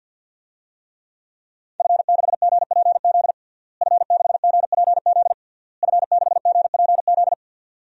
Morse code, heard as a keyed tone, about 52 wpm, 700 Hz. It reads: F6GPZ F6GPZ F6GPZ
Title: DXCC-100 Call Signs - 5 Characters - Repeated 3x - 50wpm